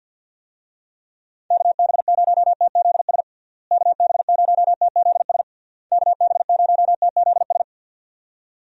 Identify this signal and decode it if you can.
Morse code, heard as a keyed tone, about 50 wpm, 700 Hz. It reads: KB0TZS KB0TZS KB0TZS
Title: USA Call Signs - 6 Characters - Repeated 3x - 50wpm